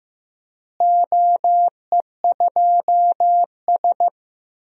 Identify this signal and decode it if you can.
Morse code, heard as a keyed tone, about 15 wpm, 700 Hz. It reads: OE2S
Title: DXCC-100 Call Signs - 4 Characters - Repeated 3x - 15wpm